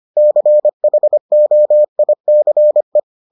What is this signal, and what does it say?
Morse code, heard as a keyed tone, about 25 wpm, 600 Hz. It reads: CHOICE